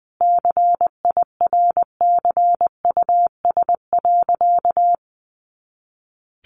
Morse code, heard as a keyed tone, about 20 wpm, 700 Hz. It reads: CIRCUS.
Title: Easy Phrases 20wpm